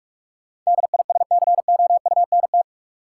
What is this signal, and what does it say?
Morse code, heard as a keyed tone, about 46 wpm, 700 Hz. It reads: DISCOUNT
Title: Sentences from Top 1200 Words - New Word Drill Alt - 45wpm